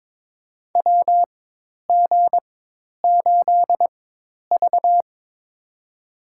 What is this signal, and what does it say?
Morse code, heard as a keyed tone, about 21 wpm, spaced as for 10 wpm, 700 Hz. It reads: WG8V
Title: USA Call Signs - 4 Characters - 4x Character Spacing - 22wpm